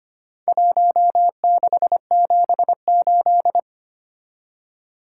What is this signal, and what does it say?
Morse code, heard as a keyed tone, about 25 wpm, 700 Hz. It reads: 1678
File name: practice / 25wpm / Four Letter-Number Combinations - 25wpm